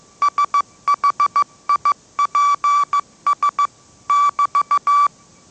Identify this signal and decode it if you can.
Morse code, heard as a keyed tone, about 15 wpm, 1160 Hz. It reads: SHIPS <BT>